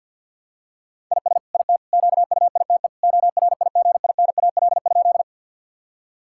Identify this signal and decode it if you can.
Morse code, heard as a keyed tone, about 46 wpm, 700 Hz. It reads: IS IT QUITE ORIGINAL?
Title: Sentences from Top 1000 Words - Review Part 6 - 50wpm